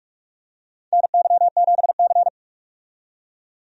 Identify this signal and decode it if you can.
Morse code, heard as a keyed tone, about 45 wpm, 700 Hz. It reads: NY7C
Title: USA Call Signs - 4 Characters - 45wpm